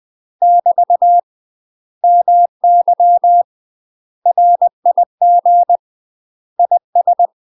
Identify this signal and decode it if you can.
Morse code, heard as a keyed tone, about 20 wpm, 700 Hz. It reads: <BT> MY RIG IS
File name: UK MASTER QSOs TRAINING / 20WPM / UK MASTER QSOs 0022